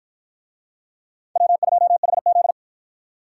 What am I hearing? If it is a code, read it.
Morse code, heard as a keyed tone, about 47 wpm, 700 Hz. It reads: W2HZ